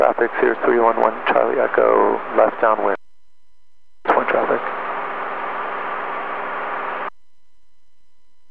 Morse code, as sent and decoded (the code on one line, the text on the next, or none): none